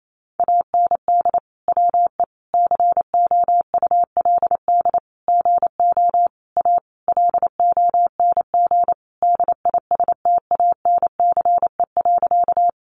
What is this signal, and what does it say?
Morse code, heard as a keyed tone, about 28 wpm, 700 Hz. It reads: AND WE COULD GO A LONG DISTANCE.
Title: Sentences from Top 1200 Words 28wpm